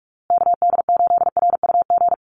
Morse code, heard as a keyed tone, about 44 wpm, 700 Hz. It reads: KD8RUG
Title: USA Call Signs Repeated 2x 45wpm